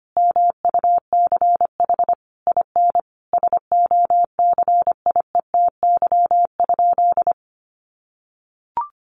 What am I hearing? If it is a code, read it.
Morse code, heard as a keyed tone, about 25 wpm, 700 Hz. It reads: MUCH IN SOCIETY?
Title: Sentences from Top 700 Words 25wpm